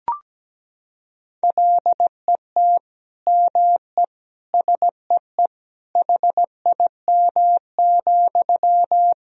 Morse code, heard as a keyed tone, about 17 wpm, 700 Hz. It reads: LET ME SEE HIM,
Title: Sentences from Top 400 Words and Abbreviations 17wpm